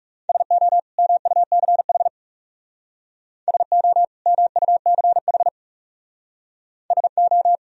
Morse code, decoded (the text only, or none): SO MUCH SO MUCH SO